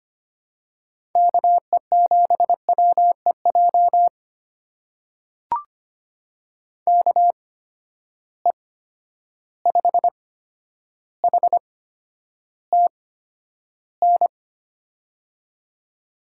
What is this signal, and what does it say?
Morse code, heard as a keyed tone, about 25 wpm, spaced as for 11 wpm, 700 Hz. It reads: KE7WEJ KE5HTN
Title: USA Call Signs - 6 Characters - 8x Character Spacing - 25wpm